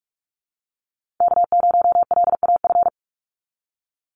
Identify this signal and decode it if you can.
Morse code, heard as a keyed tone, about 44 wpm, 700 Hz. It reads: K0LAF
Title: NA SOTA Chaser Callsigns 45wpm